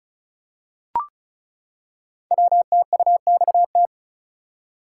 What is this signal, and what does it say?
Morse code, heard as a keyed tone, about 36 wpm, 690 Hz. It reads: WTUXT
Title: Code Groups - 5 Letters - 35wpm